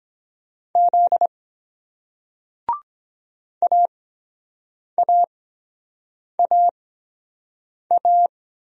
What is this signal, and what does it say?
Morse code, heard as a keyed tone, about 23 wpm, spaced as for 7 wpm, 700 Hz. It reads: Z AAAA